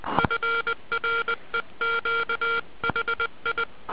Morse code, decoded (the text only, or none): FREQ HI